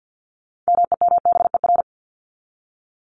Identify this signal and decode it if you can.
Morse code, heard as a keyed tone, about 50 wpm, 700 Hz. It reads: MEMBER
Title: ES:S Suffix - Top 100 Words with Base Words 50wpm